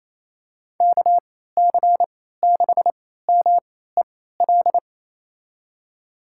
Morse code, heard as a keyed tone, about 28 wpm, spaced as for 16 wpm, 700 Hz. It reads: KC6MEL
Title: USA Call Signs - 6 Characters - 3x Character Spacing - 28wpm